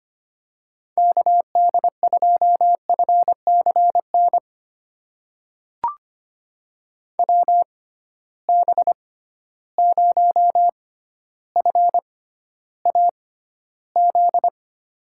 Morse code, heard as a keyed tone, about 25 wpm, spaced as for 13 wpm, 700 Hz. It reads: KD2FCN WB0FAZ
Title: USA Call Signs - 6 Characters - 6x Character Spacing - 25wpm